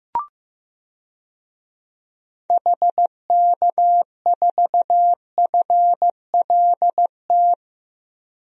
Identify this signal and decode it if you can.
Morse code, heard as a keyed tone, about 16 wpm, 700 Hz. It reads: HK4FLT
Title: DXCC-100 Call Signs - 6 Characters - 15wpm